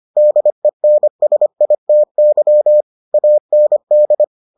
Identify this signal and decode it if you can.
Morse code, heard as a keyed tone, about 25 wpm, 600 Hz. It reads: DENSITY AND